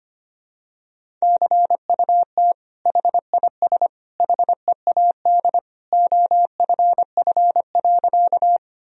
Morse code, heard as a keyed tone, about 25 wpm, 700 Hz. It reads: CUT HIS HEAD OFF.
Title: Sentences from Top 1000 Words - Review Part 1 - 25wpm